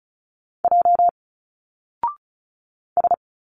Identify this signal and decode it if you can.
Morse code, heard as a keyed tone, about 34 wpm, 700 Hz. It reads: J S
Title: Single Letters 35wpm